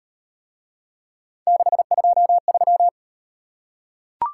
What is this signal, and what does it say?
Morse code, heard as a keyed tone, about 38 wpm, 700 Hz. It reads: 623